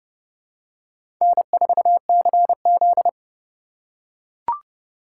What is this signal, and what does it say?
Morse code, heard as a keyed tone, about 30 wpm, 700 Hz. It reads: N4CZ
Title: USA Call Signs - 3x Character Spacing - 30wpm